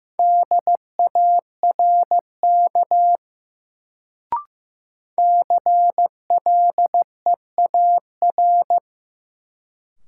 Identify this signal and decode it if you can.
Morse code, heard as a keyed tone, about 16 wpm, 700 Hz. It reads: DARK CLEAR
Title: Sentences from Top 500 Words - New Word Drill - 15wpm